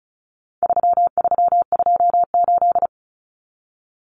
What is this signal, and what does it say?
Morse code, heard as a keyed tone, about 34 wpm, 700 Hz. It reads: 3328
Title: Common Cut Numbers - 4-Digit 35wpm